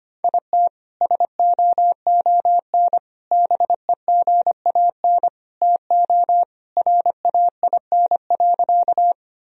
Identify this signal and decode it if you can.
Morse code, heard as a keyed tone, about 25 wpm, 700 Hz. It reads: IT SOON BEGAN TO RAIN.
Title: Sentences from Top 1000 Words - Review Part 1 - 25wpm